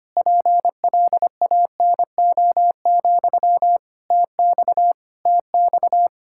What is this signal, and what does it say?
Morse code, heard as a keyed tone, about 25 wpm, 700 Hz. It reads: PLANO, TX TX
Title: Practice QSO - Exchange 1 - Beginning - 25wpm